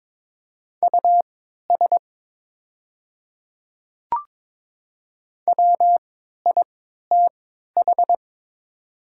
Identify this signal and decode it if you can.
Morse code, heard as a keyed tone, about 22 wpm, spaced as for 12 wpm, 700 Hz. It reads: US WITH
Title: 3x Character Spacing - Top 200 Words - 22wpm